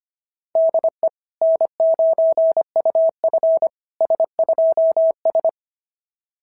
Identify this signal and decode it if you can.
Morse code, heard as a keyed tone, about 25 wpm, 690 Hz. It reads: DE N9UF S2S
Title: SOTA QSO Practice 25wpm